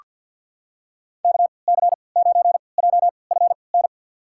Morse code, heard as a keyed tone, about 47 wpm, spaced as for 28 wpm, 700 Hz. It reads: KC9JFN
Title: USA Call Signs - 6 Characters - 3x Character Spacing - 50wpm